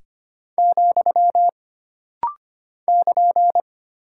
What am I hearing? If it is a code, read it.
Morse code, heard as a keyed tone, about 25 wpm, 700 Hz. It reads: , <KN>
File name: Single Punctuation-Prosign - Rapid-Fire - 25wpm